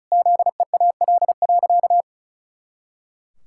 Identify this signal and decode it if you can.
Morse code, heard as a keyed tone, about 34 wpm, 700 Hz. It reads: ZEAL.
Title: Pangram Challenge v2 - 35wpm